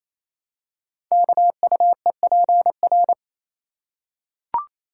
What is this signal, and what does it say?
Morse code, heard as a keyed tone, about 28 wpm, 700 Hz. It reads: KUEPR
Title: Code Groups - 5 Letters - 28wpm